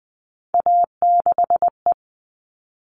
Morse code, heard as a keyed tone, about 20 wpm, 700 Hz. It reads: A6E
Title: All Cut Numbers - 3-Digit 20wpm